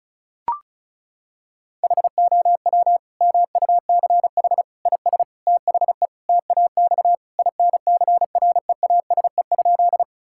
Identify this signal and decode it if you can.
Morse code, heard as a keyed tone, about 36 wpm, 700 Hz. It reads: HOW MUCH IS THE TAX INCREASE?
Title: Sentences from Top 1000 Words - Review Part 5 - 35wpm